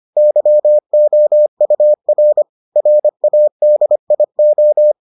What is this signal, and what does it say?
Morse code, heard as a keyed tone, about 25 wpm, 600 Hz. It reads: YOUR RADIO